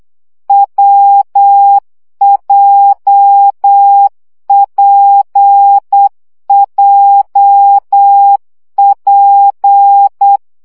Morse code, heard as a keyed tone, about 8 wpm, 800 Hz. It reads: WJPJP
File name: CW gyakorlas w p j 7 baud